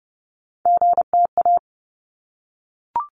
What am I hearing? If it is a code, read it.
Morse code, heard as a keyed tone, about 30 wpm, 700 Hz. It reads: GTA
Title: ARRL Sections 30wpm